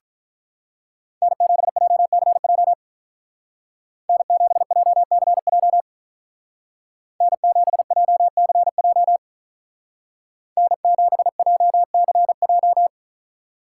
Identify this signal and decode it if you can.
Morse code, heard as a keyed tone, about 44 wpm, 700 Hz. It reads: N7JCJ N7JCJ N7JCJ N7JCJ